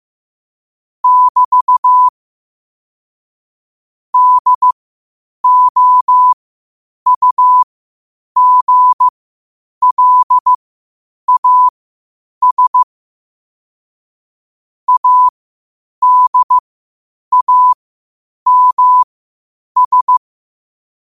Morse code, heard as a keyed tone, about 15 wpm, spaced as for 8 wpm, 1000 Hz. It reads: <BT> DOUGLAS ADAMS
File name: qotdfiles / 8wpm